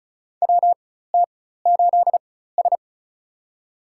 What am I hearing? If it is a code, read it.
Morse code, heard as a keyed tone, about 34 wpm, spaced as for 16 wpm, 700 Hz. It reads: WT8S
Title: USA Call Signs - 4 Characters - 4x Character Spacing - 35wpm